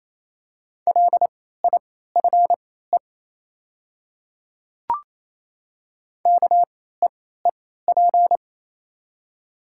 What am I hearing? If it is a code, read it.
Morse code, heard as a keyed tone, about 28 wpm, spaced as for 16 wpm, 700 Hz. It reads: LIFE KEEP
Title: 3x Character Spacing - Top 300 Words - 28wpm